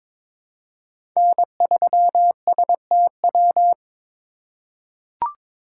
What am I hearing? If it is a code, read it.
Morse code, heard as a keyed tone, about 23 wpm, 700 Hz. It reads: N3STW
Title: USA Call Signs - 5 Characters - 4x Character Spacing - 22wpm